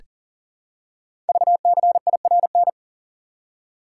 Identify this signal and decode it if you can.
Morse code, heard as a keyed tone, about 38 wpm, 700 Hz. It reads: VCIRN
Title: Code Groups - 5 Letters - 40wpm